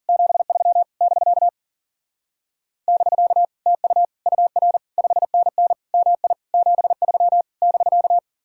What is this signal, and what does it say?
Morse code, heard as a keyed tone, about 50 wpm, 700 Hz. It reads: 73 <BK> <BK> TU UR 5NN MI 73 <BK>